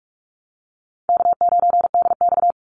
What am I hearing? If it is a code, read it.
Morse code, heard as a keyed tone, about 44 wpm, 700 Hz. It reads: K9DX
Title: Call Signs Repeated 3x - 45wpm